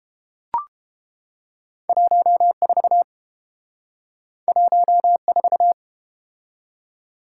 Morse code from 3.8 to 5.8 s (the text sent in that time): 14